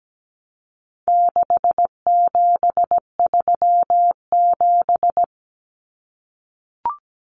Morse code, heard as a keyed tone, about 17 wpm, 700 Hz. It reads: 6737
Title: Common Cut Numbers - 4-Digit 17wpm